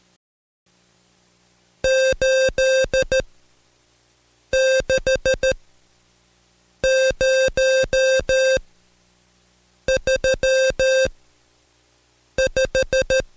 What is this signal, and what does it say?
Morse code, heard as a keyed tone, about 13 wpm, spaced as for 5 wpm, 530 Hz. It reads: 86035
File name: Code Quick / AudioDownld2 / Seg107-Day13-DayDrill 13